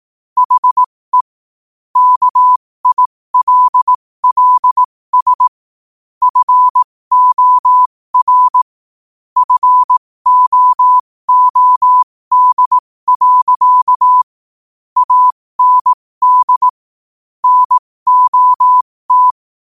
Morse code, heard as a keyed tone, about 18 wpm, spaced as for 15 wpm, 1000 Hz. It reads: HE KILLS FOR FOOD. AND NOT
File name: qotdfiles / 15wpm